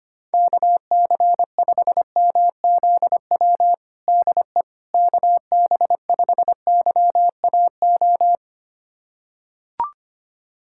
Morse code, heard as a keyed tone, about 25 wpm, 700 Hz. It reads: KC5MZW DE KB5YAO
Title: USA Callsign DE Callsign - 6 Characters - 25wpm